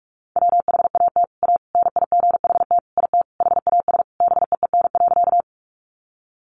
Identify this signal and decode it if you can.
Morse code, heard as a keyed tone, about 44 wpm, 700 Hz. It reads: WHAT A NIGHT IT HAS BEEN.
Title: Sentences from Top 300 Words 45wpm